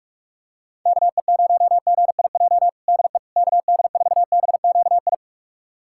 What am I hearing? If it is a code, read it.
Morse code, heard as a keyed tone, about 46 wpm, 700 Hz. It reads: KE0GIJ DE KD4BQI
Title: USA Callsign DE Callsign - 6 Characters - 45wpm